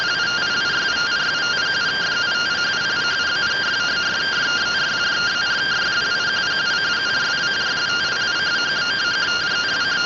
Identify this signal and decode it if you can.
Two-tone teleprinter signal, about 170 Hz shift.